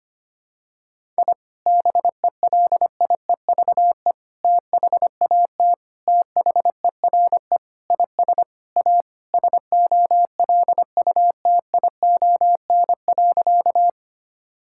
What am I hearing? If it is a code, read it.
Morse code, heard as a keyed tone, about 25 wpm, 700 Hz. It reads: I BELIEVE THAT THERE IS A SOLUTION.
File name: practice / 25wpm / Sentences with 4-Word Phrases - 25wpm